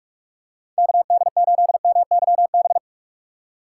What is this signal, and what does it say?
Morse code, heard as a keyed tone, about 44 wpm, 700 Hz. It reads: KD8MYB
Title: USA Call Signs - 3x Character Spacing - 45wpm